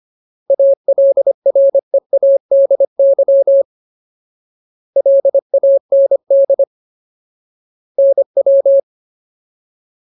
Morse code, heard as a keyed tone, about 25 wpm, 550 Hz. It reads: ALREADY LAND NW
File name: Ultimate Word List / 25WPM Ultimate